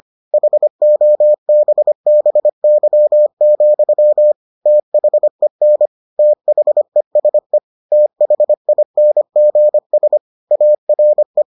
Morse code, heard as a keyed tone, about 25 wpm, 600 Hz. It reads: HOBBY, THEN THESE THINGS ARE